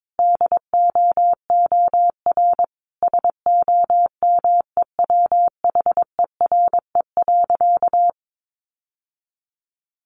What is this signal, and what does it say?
Morse code, heard as a keyed tone, about 22 wpm, 700 Hz. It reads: DOOR SOMEWHERE.
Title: Sentences from Top 1400 Words 22wpm